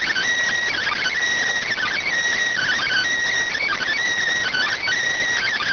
Emergency Alert System (SAME) tones, heard as harsh buzzing bursts. It is heard from 0.8 to 4.9 s.